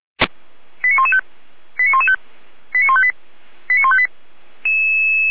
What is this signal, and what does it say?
Five-tone selective call (ZVEI), heard as quick runs of five tones, tones of about 70 ms. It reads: 89115 89115 89158 89158
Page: SDR/ZVEI